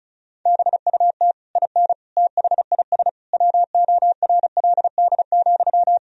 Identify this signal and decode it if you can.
Morse code, heard as a keyed tone, about 36 wpm, 700 Hz. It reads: BUT IN THIS WORLD,
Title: Sentences from Top 1000 Words - Review Part 4 - 35wpm